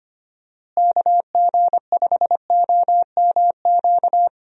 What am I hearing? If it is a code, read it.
Morse code, heard as a keyed tone, about 25 wpm, 700 Hz. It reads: KG5OMQ
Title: USA Call Signs - 6 Characters - Repeated 2x - 25wpm